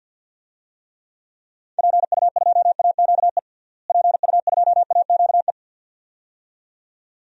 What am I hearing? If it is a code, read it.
Morse code, heard as a keyed tone, about 52 wpm, 700 Hz. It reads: PU2AQE PU2AQE
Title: DXCC-100 Call Signs - 6 Characters - Repeated 2x - 50wpm